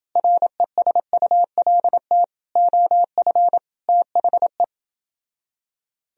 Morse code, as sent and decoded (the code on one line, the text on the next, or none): .-. ... ..- .-.. - / --- ..-. / - .... .
RESULT OF THE